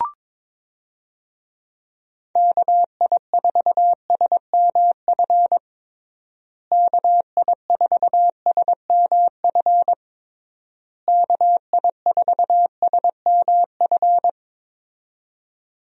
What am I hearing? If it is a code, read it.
Morse code, heard as a keyed tone, about 22 wpm, 700 Hz. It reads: KI4SMF KI4SMF KI4SMF
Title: USA Call Signs - 6 Characters - Repeated 3x - 22wpm